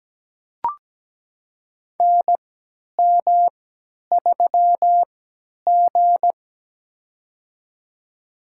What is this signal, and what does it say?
Morse code, heard as a keyed tone, about 17 wpm, spaced as for 10 wpm, 700 Hz. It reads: NM3G
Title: USA Call Signs - 4 Characters - 3x Character Spacing - 17wpm